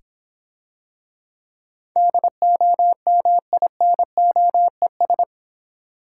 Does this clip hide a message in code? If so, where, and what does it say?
Yes: Morse code from 2.0 to 5.2 s, heard as a keyed tone, about 26 wpm, 700 Hz. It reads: DOMINOES